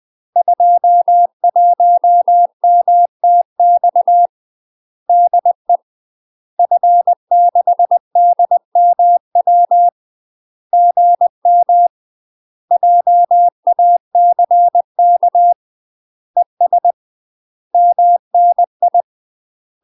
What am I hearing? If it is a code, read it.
Morse code, heard as a keyed tone, about 20 wpm, 700 Hz. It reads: 21MTX DE F6DMW GM JACK ES MNI
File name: UK MASTER QSOs TRAINING/20WPM